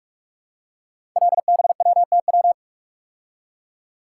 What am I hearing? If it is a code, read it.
Morse code, heard as a keyed tone, about 44 wpm, 700 Hz. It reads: LBWTW
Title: Code Groups - 5 Letter:Numbers - 45wpm